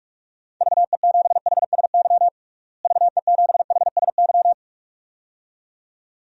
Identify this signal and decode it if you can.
Morse code, heard as a keyed tone, about 46 wpm, 700 Hz. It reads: VE7HSY VE7HSY
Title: DXCC-100 Call Signs - 6 Characters - Repeated 2x - 45wpm